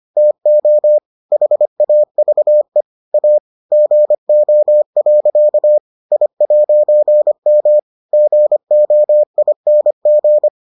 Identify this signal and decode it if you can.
Morse code, heard as a keyed tone, about 25 wpm, 600 Hz. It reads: TO HAVE A GO. I'M GOING